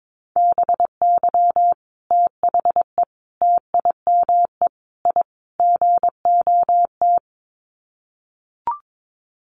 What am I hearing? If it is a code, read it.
Morse code, heard as a keyed tone, about 22 wpm, 700 Hz. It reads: BY THE TIME I GOT
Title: Sets of 5 Words Encore2 22wpm